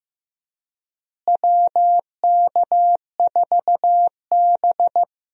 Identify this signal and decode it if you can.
Morse code, heard as a keyed tone, about 15 wpm, 700 Hz. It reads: WK4B
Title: USA Call Signs - 4 Characters - Repeated 3x - 15wpm